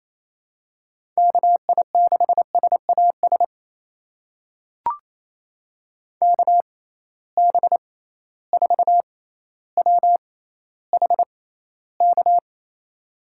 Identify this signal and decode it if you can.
Morse code, heard as a keyed tone, about 28 wpm, spaced as for 14 wpm, 700 Hz. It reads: KI6SAS KB4WHK
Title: USA Call Signs - 6 Characters - 6x Character Spacing - 28wpm